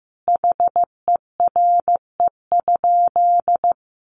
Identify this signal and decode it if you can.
Morse code, heard as a keyed tone, about 15 wpm, 700 Hz. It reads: HERE?
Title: Sentences from Top 1400 Words 15wpm